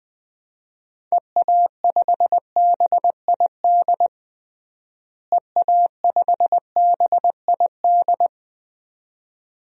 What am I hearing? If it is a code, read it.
Morse code, heard as a keyed tone, about 20 wpm, 700 Hz. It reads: EA5BID EA5BID
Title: DXCC-100 Call Signs - 6 Characters - Repeated 2x - 20wpm